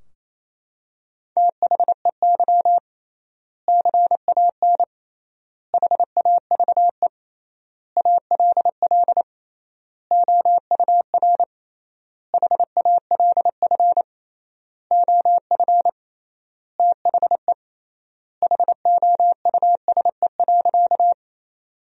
Morse code, heard as a keyed tone, about 28 wpm, 700 Hz. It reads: THEY CAN HAVE ALL OUR HALF OF THE HOUSE.
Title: 3x Word-Spacing - Sentences from Top 400 Words - 28wpm